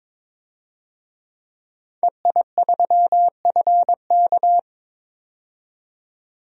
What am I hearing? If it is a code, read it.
Morse code, heard as a keyed tone, about 22 wpm, 700 Hz. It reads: EI3FK
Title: DXCC-100 Call Signs - 5 Characters - 22wpm